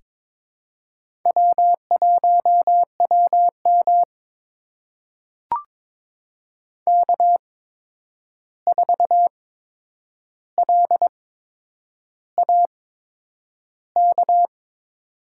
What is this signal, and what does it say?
Morse code, heard as a keyed tone, about 22 wpm, spaced as for 9 wpm, 700 Hz. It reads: W1WM K4LAK
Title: USA Call Signs - 8x Character Spacing - 22wpm